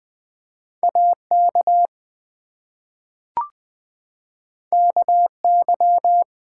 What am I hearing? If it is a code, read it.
Morse code, heard as a keyed tone, about 20 wpm, 700 Hz. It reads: AK KY
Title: ARRL Sections v2 - 20wpm